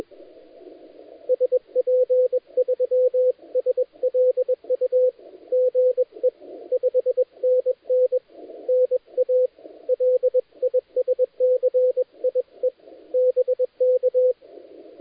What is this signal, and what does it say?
Morse code, heard as a keyed tone, about 21 wpm, 510 Hz, with radio noise. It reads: SP3SLU GE 5NN NA LISCIE BK
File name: dlazaawansowanych / nagraniazpasma / titawaNETcontrol